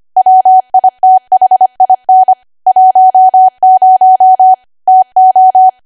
Morse code, heard as a keyed tone, about 25 wpm, 750 Hz. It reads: WITHIN 10 TO
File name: Archive / 25 WPM / 25WPM